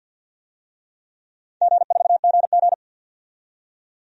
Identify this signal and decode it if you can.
Morse code, heard as a keyed tone, about 53 wpm, 700 Hz. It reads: G4GG
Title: DXCC-100 Call Signs - 4 Characters - 50wpm